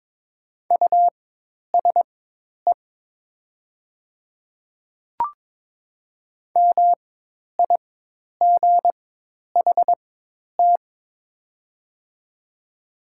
Morse code, heard as a keyed tone, about 22 wpm, spaced as for 10 wpm, 700 Hz. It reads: USE MIGHT
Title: 4x Character Spacing - Top 300 Words - 22wpm